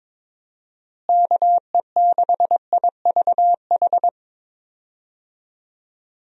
Morse code, heard as a keyed tone, about 22 wpm, 700 Hz. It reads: KE6IVH
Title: USA Call Signs - 6 Characters - Repeated 2x - 22wpm